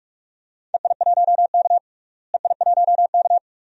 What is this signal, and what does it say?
Morse code, heard as a keyed tone, about 41 wpm, 700 Hz. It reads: EI1K EI1K
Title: DXCC-100 Call Signs - 4 Characters - Repeated 2x - 45wpm